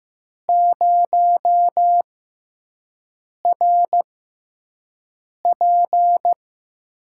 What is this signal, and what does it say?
Morse code, heard as a keyed tone, about 15 wpm, spaced as for 5 wpm, 700 Hz. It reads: 0RP
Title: USA Call Signs - 6x Character Spacing - 15wpm